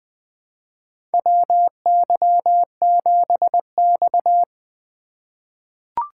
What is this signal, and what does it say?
Morse code, heard as a keyed tone, about 20 wpm, 700 Hz. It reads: WY7X
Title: USA Call Signs - 4 Characters - 3x Character Spacing - 20wpm